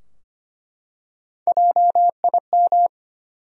Morse code, heard as a keyed tone, about 25 wpm, 700 Hz. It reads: JIM
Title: USA Common Names - Max 4 Letters - 25wpm